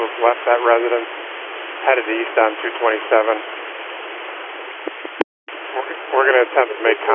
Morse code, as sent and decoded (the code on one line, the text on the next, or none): none